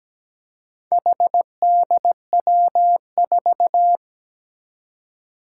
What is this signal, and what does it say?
Morse code, heard as a keyed tone, about 17 wpm, 700 Hz. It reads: HDW4
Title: Four Challenging Letter-Number Combinations 17wpm